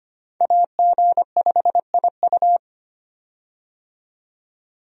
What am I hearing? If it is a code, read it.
Morse code, heard as a keyed tone, about 25 wpm, 700 Hz. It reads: AG5IU
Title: USA Call Signs - 5 Characters - 25wpm